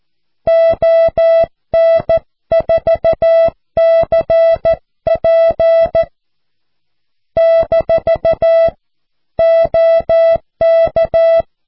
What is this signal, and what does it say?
Morse code, heard as a keyed tone, about 14 wpm, 660 Hz. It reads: ON4CP - OK
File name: cw / QSO